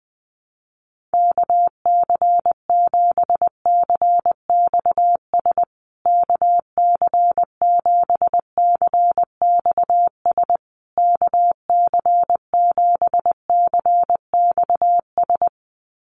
Morse code, heard as a keyed tone, about 20 wpm, 700 Hz. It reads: KC7CXS KC7CXS KC7CXS
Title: USA Call Signs Repeated 3x 20wpm